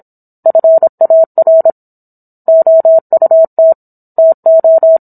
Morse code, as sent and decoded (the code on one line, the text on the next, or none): ..-. .- .-. / --- ..- - / - ---
FAR OUT TO